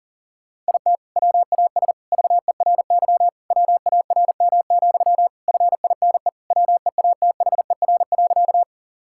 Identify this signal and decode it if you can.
Morse code, heard as a keyed tone, about 40 wpm, 690 Hz. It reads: IT WAS VERY WARM, FINE WEATHER.
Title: Sentences from Top 1000 Words - Review Part 5 - 40wpm